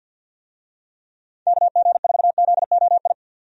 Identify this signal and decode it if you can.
Morse code, heard as a keyed tone, about 52 wpm, 700 Hz. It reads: KG4ZOI
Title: USA Call Signs - 6 Characters - 50wpm